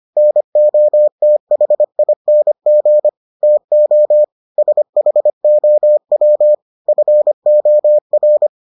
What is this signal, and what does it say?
Morse code, heard as a keyed tone, about 25 wpm, 600 Hz. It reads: NOTHING TO SHOW FOR